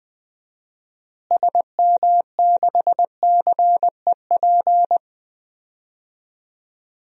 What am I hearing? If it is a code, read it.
Morse code, heard as a keyed tone, about 20 wpm, 700 Hz. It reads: SM6CEP
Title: DXCC-100 Call Signs - 6 Characters - Repeated 3x - 20wpm